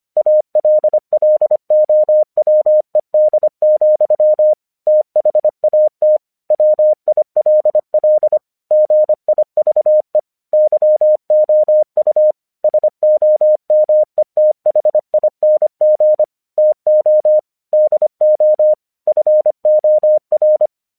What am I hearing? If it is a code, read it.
Morse code, heard as a keyed tone, about 26 wpm, 610 Hz. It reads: ALLOWED, THAT WILL GIVE YOU SOMETHING TO DO FOR
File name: 1 English 25wpm / AQC21